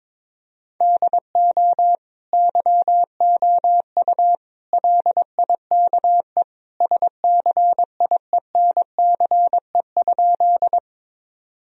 Morse code, heard as a keyed tone, about 22 wpm, 700 Hz. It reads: DO YOU LIKE SCIENCE?
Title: Sentences from Top 1000 Words - Review Part 2 - 22wpm